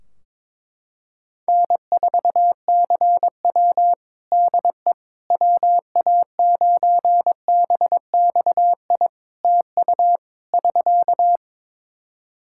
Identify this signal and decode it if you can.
Morse code, heard as a keyed tone, about 22 wpm, 700 Hz. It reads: N4CW DE WA9BXI TU <SK>
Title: Practice QSO - Exchange 4 - Ending - 22wpm